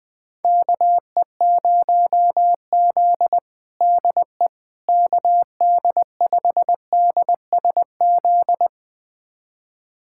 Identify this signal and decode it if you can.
Morse code, heard as a keyed tone, about 20 wpm, 700 Hz. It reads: KE0Z DE KD5DSZ
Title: USA Callsign DE Callsign 20wpm